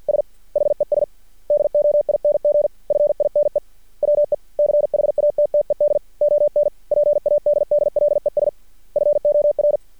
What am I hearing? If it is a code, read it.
Morse code, heard as a keyed tone, about 50 wpm, 590 Hz. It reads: S HES DOING FINE WE CHATTED ON PADDLES FOR
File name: archived / CRSnet